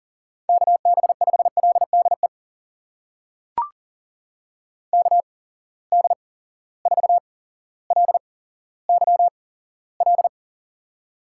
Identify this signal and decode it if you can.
Morse code, heard as a keyed tone, about 40 wpm, spaced as for 17 wpm, 700 Hz. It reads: KB5LDE KD4LYL
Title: USA Call Signs - 6 Characters - 8x Character Spacing - 40wpm